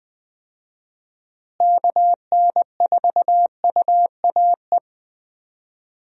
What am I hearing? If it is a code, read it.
Morse code, heard as a keyed tone, about 21 wpm, 700 Hz. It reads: KN4UAE
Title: USA Call Signs - 6 Characters - Repeated 3x - 20wpm